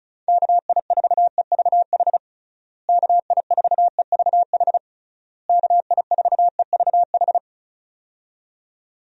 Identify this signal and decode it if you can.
Morse code, heard as a keyed tone, about 34 wpm, 700 Hz. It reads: KI4EVH KI4EVH KI4EVH
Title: USA Call Signs - 6 Characters - Repeated 3x - 35wpm